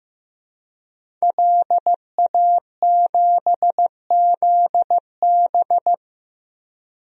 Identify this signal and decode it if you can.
Morse code, heard as a keyed tone, about 15 wpm, 700 Hz. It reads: LA7ZB
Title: DXCC-100 Call Signs - 5 Characters - 15wpm